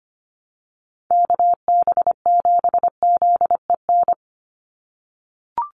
Morse code, heard as a keyed tone, about 25 wpm, 700 Hz. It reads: KB7ZEN
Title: USA Call Signs 25wpm